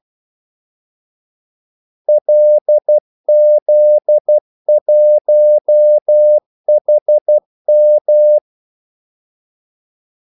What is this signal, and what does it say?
Morse code, heard as a keyed tone, about 12 wpm, 600 Hz. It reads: LZ1HM
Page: 2 ekstra ordmellemrum mellem kaldesignalerne